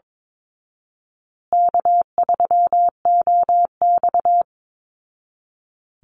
Morse code, heard as a keyed tone, about 22 wpm, 700 Hz. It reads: K3OX
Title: Easy Call Signs Repeated 3x - 22wpm